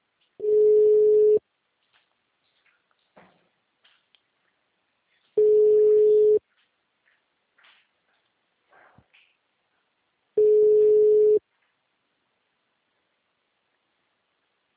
Ringback tone (Europe, 425 Hz).